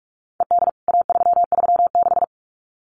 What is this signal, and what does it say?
Morse code, heard as a keyed tone, about 44 wpm, 700 Hz. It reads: ED A336
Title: CWT Exchange 45wpm